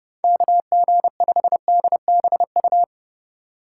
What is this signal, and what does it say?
Morse code, heard as a keyed tone, about 30 wpm, 700 Hz. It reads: KG5DBU
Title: USA Call Signs - 6 Characters - 4x Character Spacing - 30wpm